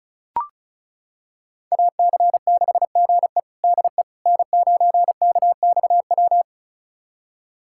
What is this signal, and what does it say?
Morse code, heard as a keyed tone, about 34 wpm, 700 Hz. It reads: AC6GE DE N9KXW
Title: USA Callsign DE Callsign - 5 Characters - 35wpm